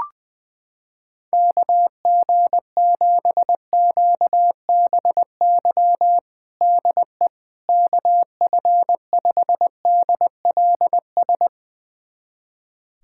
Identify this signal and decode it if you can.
Morse code, heard as a keyed tone, about 21 wpm, 700 Hz. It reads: KG7QBY DE KF5DLS
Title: USA Callsign DE Callsign - 6 Characters - 20wpm